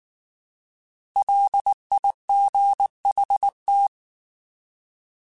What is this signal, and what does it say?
Morse code, heard as a keyed tone, about 19 wpm, 800 Hz. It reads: LIGHT
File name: Alphabetical / Alphabetical mp3s / Ls @18